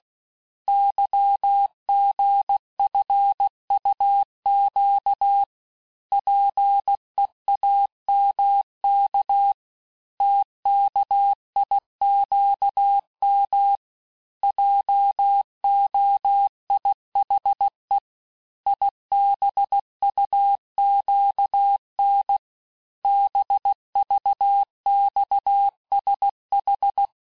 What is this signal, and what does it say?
Morse code, heard as a keyed tone, about 16 wpm, 800 Hz. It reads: YGFUQ PEAMK TKIQM JOIHE IBUQN BVXSH
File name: cours-du-lundi files / Cours-1586-texte-chiffre-vitesse-15mpm